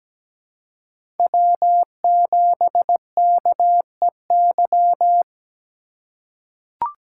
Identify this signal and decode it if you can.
Morse code, heard as a keyed tone, about 17 wpm, 700 Hz. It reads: W7KEY
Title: USA Call Signs - 5 Characters - 2x Character Spacing - 17wpm